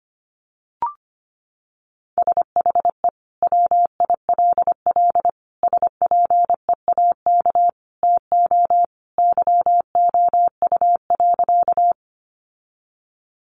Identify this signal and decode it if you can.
Morse code, heard as a keyed tone, about 25 wpm, 700 Hz. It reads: SHE WILL SPEAK TO YOU.
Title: Sentences from Top 1100 Words 25wpm